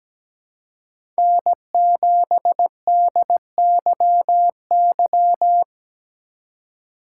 Morse code, heard as a keyed tone, about 17 wpm, 700 Hz. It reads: N7DYY
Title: USA Call Signs - 5 Characters - 3x Character Spacing - 17wpm